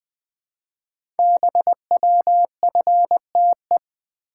Morse code, heard as a keyed tone, about 20 wpm, 700 Hz. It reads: BWFTE
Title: Code Groups - 5 Letters - 20wpm